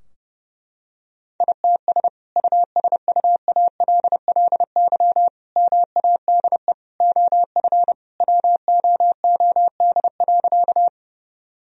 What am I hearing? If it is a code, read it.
Morse code, heard as a keyed tone, about 32 wpm, 700 Hz. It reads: ITS USUALLY MADE OF WOOD.
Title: Sentences from Top 1000 Words - Review Part 3 - 30wpm